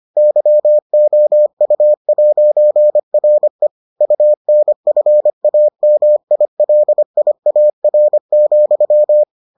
Morse code, heard as a keyed tone, about 25 wpm, 600 Hz. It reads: YOU'RE UNFAMILIAR,